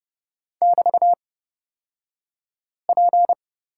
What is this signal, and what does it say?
Morse code, heard as a keyed tone, about 30 wpm, spaced as for 5 wpm, 690 Hz. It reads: <BT>P
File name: Lesson 36d - BT - Two Characters 30wpm